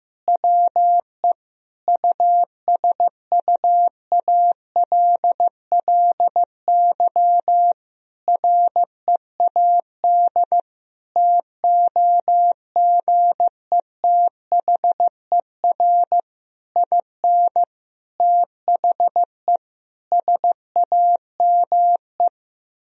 Morse code, heard as a keyed tone, about 15 wpm, 700 Hz. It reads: WE USUALLY READ TOGETHER IN THE SAME
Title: Sentences from Top 1000 Words - Review Part 3 - 15wpm